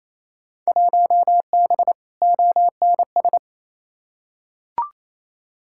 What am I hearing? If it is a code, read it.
Morse code, heard as a keyed tone, about 28 wpm, 700 Hz. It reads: 1B ONS
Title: Field Day Exchanges Encore - 28wpm